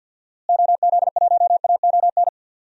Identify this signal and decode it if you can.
Morse code, heard as a keyed tone, about 52 wpm, 700 Hz. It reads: OZ1AON
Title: DXCC-100 Call Signs - 6 Characters - Repeated 3x - 50wpm